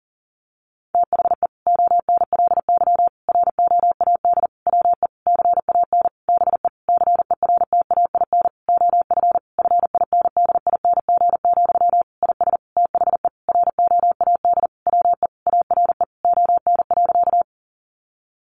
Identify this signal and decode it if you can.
Morse code, heard as a keyed tone, about 40 wpm, 700 Hz. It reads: THE ONLY ROAD WE CAN BE CERTAIN OF FINDING, IS THE ROAD WE ARE ON.
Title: Sentences from Top 1100 Words 40wpm